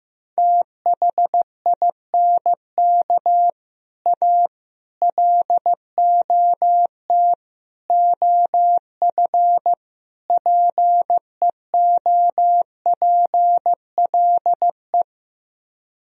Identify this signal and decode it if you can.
Morse code, heard as a keyed tone, about 15 wpm, 700 Hz. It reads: THINK A LOT OF PEOPLE
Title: Speed-Racing - Sets of 5 Words 15wpm